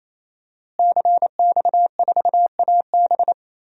Morse code, heard as a keyed tone, about 27 wpm, 700 Hz. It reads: CX4AB
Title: DXCC-100 Call Signs - 5 Characters - Repeated 3x - 28wpm